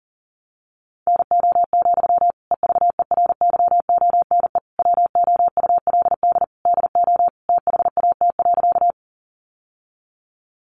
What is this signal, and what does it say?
Morse code, heard as a keyed tone, about 40 wpm, 700 Hz. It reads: NO, EVERYONE WOULD DO THAT.
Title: Sentences from Top 800 Words 40wpm